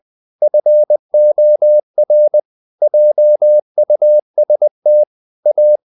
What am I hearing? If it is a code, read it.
Morse code, heard as a keyed tone, about 20 wpm, 600 Hz. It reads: FOR JUST A
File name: the old man and the sea part 2 20wpm